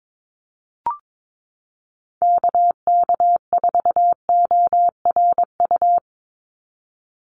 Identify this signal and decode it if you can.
Morse code, heard as a keyed tone, about 22 wpm, 700 Hz. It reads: KK4ORU
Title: USA Call Signs 22wpm